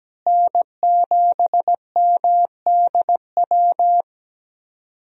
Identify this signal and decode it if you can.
Morse code, heard as a keyed tone, about 17 wpm, 700 Hz. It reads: N7MDW
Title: USA Call Signs - 5 Characters - Repeated 2x - 17wpm